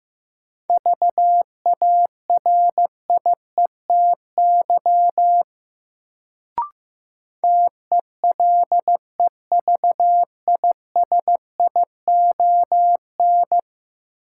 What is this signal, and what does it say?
Morse code, heard as a keyed tone, about 15 wpm, 700 Hz. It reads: VARIETY TELEVISION
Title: Sentences from Top 1000 Words - Review Part 4 - New Word Drill - 15wpm